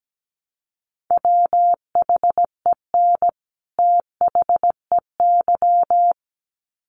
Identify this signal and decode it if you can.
Morse code, heard as a keyed tone, about 17 wpm, 700 Hz. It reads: WHEN THEY
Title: Sets of 2 Words 17wpm